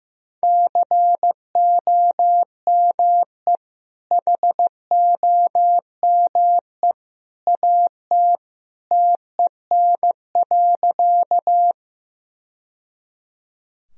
Morse code, heard as a keyed tone, about 15 wpm, 700 Hz. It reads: COME HOME AT TEN.
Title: Sentences from Top 1000 Words - Review Part 2 - 15wpm